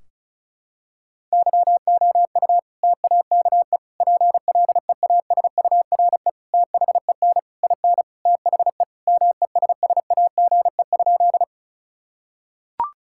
Morse code, heard as a keyed tone, about 34 wpm, 700 Hz. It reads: YOU TAKE PLEASURE THEN IN THE MESSAGE?
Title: Sentences from Top 1600 Words 35wpm